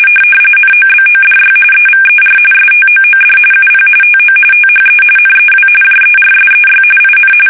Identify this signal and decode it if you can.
RTTY, about 45.45 baud. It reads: NOBODY EVER STOOD UP TO THE NEW JERSEY IN
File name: fsk850 / battleshipnj